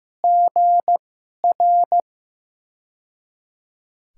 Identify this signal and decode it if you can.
Morse code, heard as a keyed tone, about 15 wpm, spaced as for 11 wpm, 700 Hz. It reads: GR
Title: USA Call Signs - 6 Characters - 2x Character Spacing - 15wpm